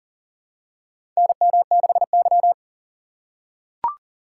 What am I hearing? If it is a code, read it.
Morse code, heard as a keyed tone, about 40 wpm, 700 Hz. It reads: NM6Y
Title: USA Call Signs - 4 Characters - 3x Character Spacing - 40wpm